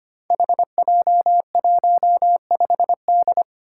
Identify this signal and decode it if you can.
Morse code, heard as a keyed tone, about 25 wpm, 700 Hz. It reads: HJ15D
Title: Five Challenging Letter-Number Combinations 25wpm